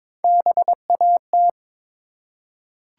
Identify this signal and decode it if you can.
Morse code, heard as a keyed tone, about 22 wpm, 700 Hz. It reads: BAT
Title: Three Letter Words - 22wpm